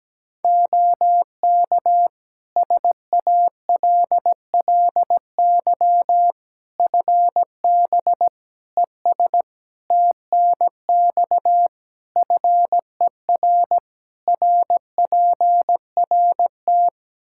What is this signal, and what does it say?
Morse code, heard as a keyed tone, about 17 wpm, 700 Hz. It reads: OK SALLY FB ES TNX FER RPRT
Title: Practice QSO - Exchange 2 - Info - 17wpm